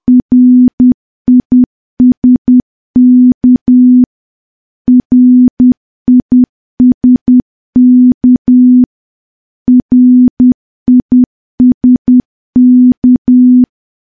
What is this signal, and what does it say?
Morse code, heard as a keyed tone, about 10 wpm, 250 Hz. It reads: RISK RISK RISK
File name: morse / morse-training-10WPM